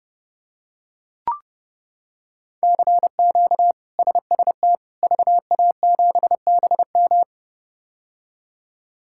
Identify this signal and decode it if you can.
Morse code, heard as a keyed tone, about 30 wpm, 700 Hz. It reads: CQ SST VA7BM
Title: K1USN SST QSO Practice - 30wpm